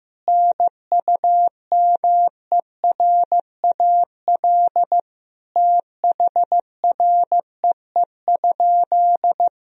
Morse code, heard as a keyed tone, about 15 wpm, 700 Hz. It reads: NUMERAL THREE?
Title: Sentences from Top 1000 Words - Review Part 2 - 15wpm